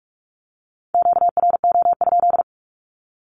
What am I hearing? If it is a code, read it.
Morse code, heard as a keyed tone, about 46 wpm, 700 Hz. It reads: QRO?
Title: Q-Codes with Spoken Meaning - 45wpm